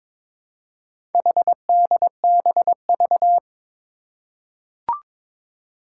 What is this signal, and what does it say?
Morse code, heard as a keyed tone, about 22 wpm, 700 Hz. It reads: HDBV
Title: Four Challenging Letter-Number Combinations 22wpm